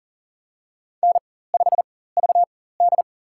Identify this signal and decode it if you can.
Morse code, heard as a keyed tone, about 40 wpm, spaced as for 19 wpm, 700 Hz. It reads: N5VD
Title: USA Call Signs - 4 Characters - 4x Character Spacing - 40wpm